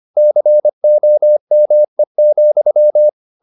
Morse code, heard as a keyed tone, about 25 wpm, 600 Hz. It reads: COME,
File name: the old man and the sea part 1 25wpm